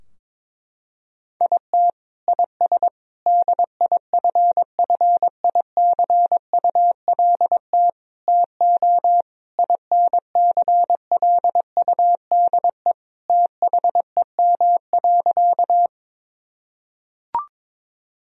Morse code, heard as a keyed tone, about 22 wpm, 700 Hz. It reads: IT IS DIFFICULT TO INCLUDE THEM.
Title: Sentences from Top 1000 Words - Review Part 3 - 22wpm